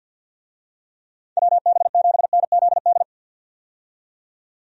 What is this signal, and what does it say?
Morse code, heard as a keyed tone, about 47 wpm, 700 Hz. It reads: WB7NZD
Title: USA Call Signs - 6 Characters - 50wpm